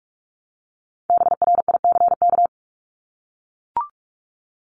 Morse code, heard as a keyed tone, about 44 wpm, 700 Hz. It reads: BRICK
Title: Top 1751-2000 Words 45wpm